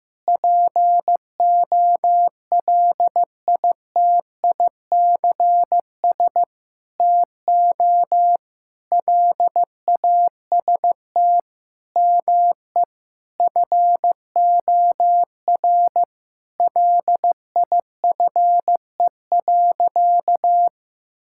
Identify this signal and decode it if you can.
Morse code, heard as a keyed tone, about 15 wpm, 700 Hz. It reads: POLITICS TO LAST ME FOR LIFE.
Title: Sentences from Top 1000 Words - Review Part 6 - 15wpm